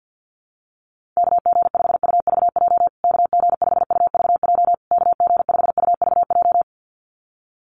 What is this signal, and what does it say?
Morse code, heard as a keyed tone, about 50 wpm, 700 Hz. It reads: KG5UVJ KG5UVJ KG5UVJ
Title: USA Call Signs Repeated 3x 50wpm